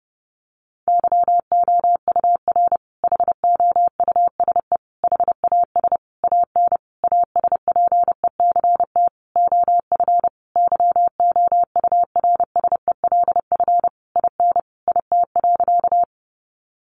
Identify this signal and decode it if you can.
Morse code, heard as a keyed tone, about 28 wpm, 700 Hz. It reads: YOUR HOUSE HAS AN ASPECT OF YOURSELF IN IT.
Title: Sentences from Top 1300 Words 30wpm